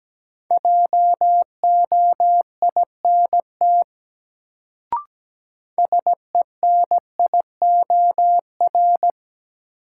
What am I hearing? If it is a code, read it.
Morse code, heard as a keyed tone, about 17 wpm, 700 Hz. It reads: JOINT SENIOR
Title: Sets of 5 Words Encore - New Word Drill - 17wpm